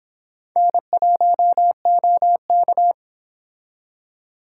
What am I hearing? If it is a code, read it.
Morse code, heard as a keyed tone, about 26 wpm, 700 Hz. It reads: N1OK